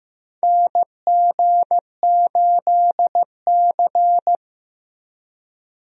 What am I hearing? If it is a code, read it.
Morse code, heard as a keyed tone, about 15 wpm, 700 Hz. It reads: NG8C
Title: USA Call Signs - 4 Characters - 15wpm